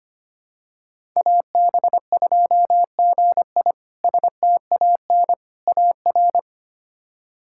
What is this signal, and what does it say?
Morse code, heard as a keyed tone, about 25 wpm, 700 Hz. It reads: AB2GI STAN AR
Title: K1USN SST QSO Practice - 25wpm